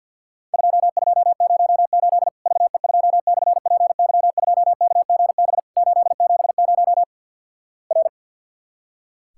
Morse code, heard as a keyed tone, about 50 wpm, 700 Hz. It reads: 2208 VE2/PY2KGB 870 R
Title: Worked All Europe DX Contest - QTC Exchange - 50wpm